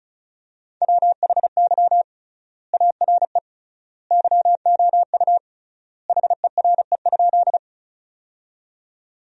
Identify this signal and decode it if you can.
Morse code, heard as a keyed tone, about 34 wpm, 700 Hz. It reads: WHY ARE YOU HERE?
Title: 3x Word-Spacing - Sentences from Top 200 Words - 35wpm